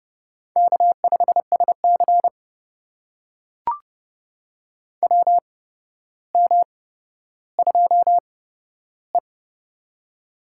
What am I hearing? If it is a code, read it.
Morse code, heard as a keyed tone, about 29 wpm, spaced as for 13 wpm, 700 Hz. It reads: K5SC WM2E
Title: USA Call Signs - 4 Characters - 8x Character Spacing - 30wpm